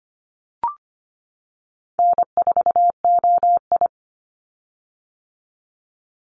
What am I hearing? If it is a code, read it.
Morse code, heard as a keyed tone, about 25 wpm, 700 Hz. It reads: N4OI
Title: Call Signs - 25wpm